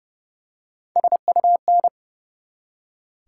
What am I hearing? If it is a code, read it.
Morse code, heard as a keyed tone, about 32 wpm, 700 Hz. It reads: SUN
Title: Three Letter Words - 30wpm